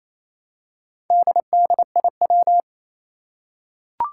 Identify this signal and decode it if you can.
Morse code, heard as a keyed tone, about 28 wpm, 700 Hz. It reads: DDIW
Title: Four Challenging Letter-Number Combinations 28wpm